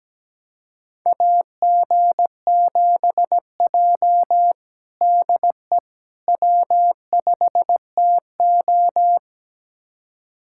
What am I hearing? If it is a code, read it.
Morse code, heard as a keyed tone, about 17 wpm, 700 Hz. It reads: AG7J DE W5TO
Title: USA Callsign DE Callsign - 4 Characters - 17wpm